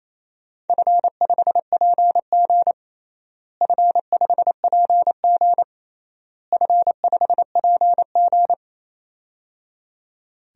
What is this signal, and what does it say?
Morse code, heard as a keyed tone, about 28 wpm, 700 Hz. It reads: F5PG F5PG F5PG
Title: DXCC-100 Call Signs - 4 Characters - Repeated 3x - 28wpm